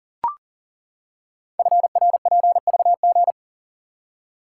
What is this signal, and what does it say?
Morse code, heard as a keyed tone, about 41 wpm, 690 Hz. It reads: FRPVG
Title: Code Groups - 5 Letters - 40wpm